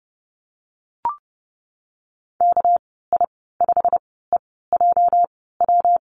Morse code, heard as a keyed tone, about 30 wpm, spaced as for 17 wpm, 700 Hz. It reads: KI5EJW
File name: Lesson 37g - Letter x - Call Signs 30wpm